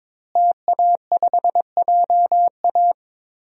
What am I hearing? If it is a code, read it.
Morse code, heard as a keyed tone, about 22 wpm, 700 Hz. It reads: TA5JA
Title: DXCC-100 Call Signs - 5 Characters - Repeated 2x - 22wpm